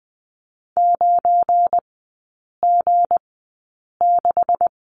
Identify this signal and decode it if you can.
Morse code, heard as a keyed tone, about 20 wpm, spaced as for 8 wpm, 700 Hz. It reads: 9G6
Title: 172 Lesson 28e - Number 9 - Three Characters 20wpm